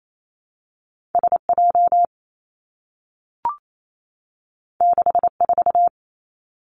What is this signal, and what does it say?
Morse code, heard as a keyed tone, about 28 wpm, 700 Hz. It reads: S J 6 4
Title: Two Challenging Letter-Number Combinations 28wpm